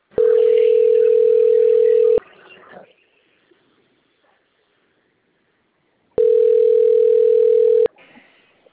Ringback tone (North America), 440 + 480 Hz.
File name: out- - - - - .38423